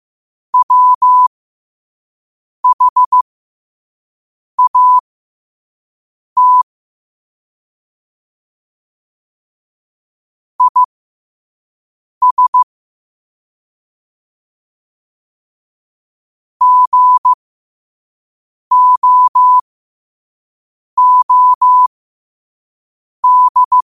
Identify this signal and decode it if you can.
Morse code, heard as a keyed tone, about 15 wpm, spaced as for 5 wpm, 1000 Hz. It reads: WHAT IS GOOD
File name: qotdfiles / 5wpm